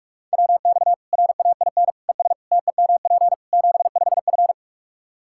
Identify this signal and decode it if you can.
Morse code, heard as a keyed tone, about 41 wpm, 700 Hz. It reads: WX RAIN ES TEMP 75F
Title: Practice QSO - Exchange 2 - Info - 45wpm